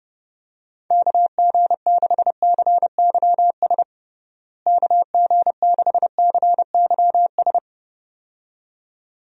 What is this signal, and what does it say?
Morse code, heard as a keyed tone, about 30 wpm, 700 Hz. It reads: KG6CYS KG6CYS
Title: USA Call Signs - 6 Characters - Repeated 2x - 30wpm